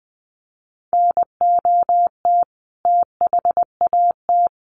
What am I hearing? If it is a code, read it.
Morse code, heard as a keyed tone, about 20 wpm, 700 Hz. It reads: NOT THAT
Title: A Sound Of Thunder 20wpm